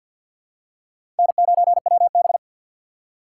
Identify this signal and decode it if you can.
Morse code, heard as a keyed tone, about 52 wpm, 700 Hz. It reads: N9WB
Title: USA Call Signs - 4 Characters - 2x Character Spacing - 50wpm